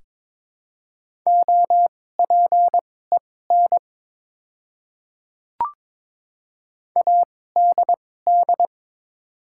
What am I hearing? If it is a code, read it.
Morse code, heard as a keyed tone, about 23 wpm, spaced as for 16 wpm, 700 Hz. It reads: OPEN ADD
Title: 2x Character Spacing - Top 300 Words - 22wpm